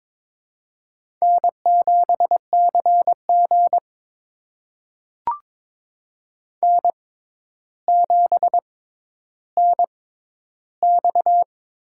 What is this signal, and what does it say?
Morse code, heard as a keyed tone, about 22 wpm, spaced as for 11 wpm, 700 Hz. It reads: N7CG N7NX
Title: USA Call Signs - 6x Character Spacing - 22wpm